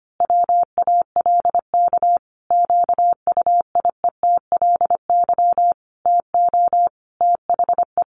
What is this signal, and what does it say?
Morse code, heard as a keyed tone, about 25 wpm, 700 Hz. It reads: WALK QUIETLY TO THE
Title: A Sound Of Thunder 25wpm